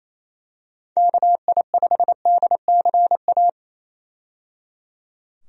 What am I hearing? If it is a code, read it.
Morse code, heard as a keyed tone, about 27 wpm, 700 Hz. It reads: KI5DCA
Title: USA Call Signs - 6 Characters - 28wpm